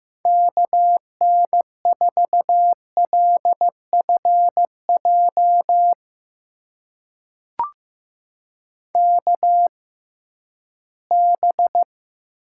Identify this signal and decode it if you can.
Morse code, heard as a keyed tone, about 15 wpm, spaced as for 11 wpm, 700 Hz. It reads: KN4LFJ KB
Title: USA Call Signs - 6 Characters - 6x Character Spacing - 15wpm